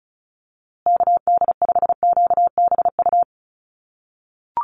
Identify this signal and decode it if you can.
Morse code, heard as a keyed tone, about 34 wpm, 700 Hz. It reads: KD5QBU
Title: USA Call Signs Repeated 3x 35wpm